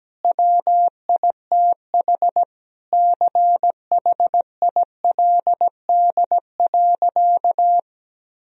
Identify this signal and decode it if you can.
Morse code, heard as a keyed tone, about 17 wpm, 700 Hz. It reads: WITH CHILD.
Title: Sentences from Top 1000 Words - Review Part 1 - 17wpm